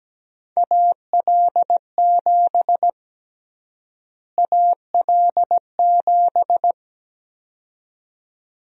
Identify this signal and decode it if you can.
Morse code, heard as a keyed tone, about 17 wpm, 700 Hz. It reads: AL7 AL7
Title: USA Call Sign Prefixes - Repeated 2x - 17wpm